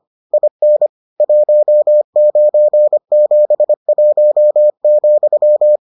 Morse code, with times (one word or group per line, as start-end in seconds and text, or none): 0.3-0.9 IN
1.2-5.8 1971,